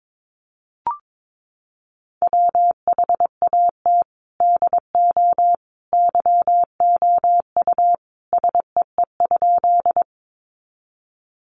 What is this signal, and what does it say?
Morse code, heard as a keyed tone, about 22 wpm, 700 Hz. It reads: WHAT DO YOU SEE?
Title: Sentences from Top 100 Words 22wpm